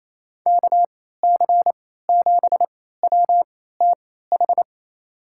Morse code, heard as a keyed tone, about 28 wpm, spaced as for 16 wpm, 700 Hz. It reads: KC7WTH
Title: USA Call Signs - 6 Characters - 3x Character Spacing - 28wpm